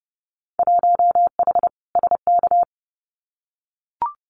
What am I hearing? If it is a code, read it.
Morse code, heard as a keyed tone, about 30 wpm, 700 Hz. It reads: 1H SK
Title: Winter Field Day Exchanges 30wpm